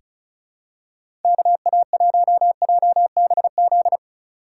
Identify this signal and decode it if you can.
Morse code, heard as a keyed tone, about 36 wpm, 700 Hz. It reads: KA1JBZ
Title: USA Call Signs - 6 Characters - 4x Character Spacing - 35wpm